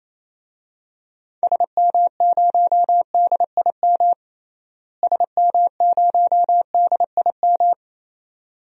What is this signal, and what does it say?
Morse code, heard as a keyed tone, about 28 wpm, 700 Hz. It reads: SM0DIM SM0DIM
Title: DXCC-100 Call Signs - 6 Characters - Repeated 2x - 28wpm